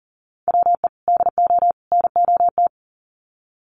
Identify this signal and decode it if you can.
Morse code, heard as a keyed tone, about 37 wpm, 700 Hz. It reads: WE DO NOT
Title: Sets of 3 Words 40wpm